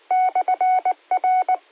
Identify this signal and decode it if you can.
Morse code, heard as a keyed tone, about 19 wpm, 740 Hz. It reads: /R
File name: qsorec RepeaterLogic